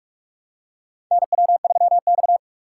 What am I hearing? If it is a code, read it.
Morse code, heard as a keyed tone, about 46 wpm, 700 Hz. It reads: NW3X
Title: USA Call Signs - 4 Characters - 4x Character Spacing - 45wpm